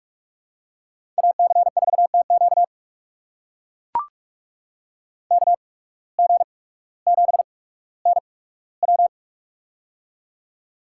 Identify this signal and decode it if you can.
Morse code, heard as a keyed tone, about 42 wpm, spaced as for 19 wpm, 700 Hz. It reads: AC4TQ KG7NW
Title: USA Call Signs - 5 Characters - 8x Character Spacing - 45wpm